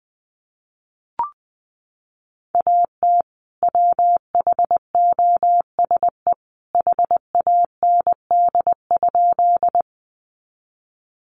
Morse code, heard as a keyed tone, about 20 wpm, 700 Hz. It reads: AT WHOSE HAND?
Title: Sentences from Top 1200 Words 20wpm